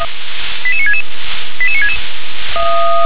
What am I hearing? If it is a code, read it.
Five-tone selective call (ZVEI), heard as quick runs of five tones, tones of about 70 ms. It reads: 88966 88966
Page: SDR/ZVEI